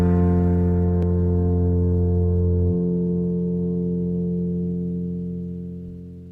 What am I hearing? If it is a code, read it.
Dial tone (North America), 350 + 440 Hz, with noise.
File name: wrct-log hi- 04